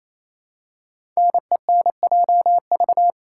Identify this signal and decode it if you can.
Morse code, heard as a keyed tone, about 28 wpm, 700 Hz. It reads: NENJV
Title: Code Groups - 5 Letters - 28wpm